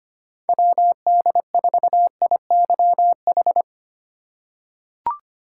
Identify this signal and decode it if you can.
Morse code, heard as a keyed tone, about 25 wpm, 700 Hz. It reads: WD4IYH
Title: USA Call Signs - 3x Character Spacing - 25wpm